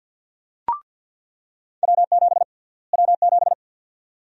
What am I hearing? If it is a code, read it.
Morse code, heard as a keyed tone, about 47 wpm, 700 Hz. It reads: W7 W7
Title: USA Call Sign Prefixes - Repeated 2x - 50wpm